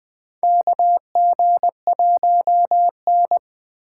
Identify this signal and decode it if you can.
Morse code, heard as a keyed tone, about 20 wpm, 700 Hz. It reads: KG1N
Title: Four Letter-Number Combinations - 20wpm